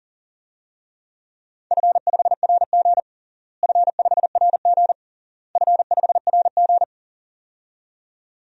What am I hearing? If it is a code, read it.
Morse code, heard as a keyed tone, about 40 wpm, 700 Hz. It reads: F5RG F5RG F5RG
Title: DXCC-100 Call Signs - 4 Characters - Repeated 3x - 40wpm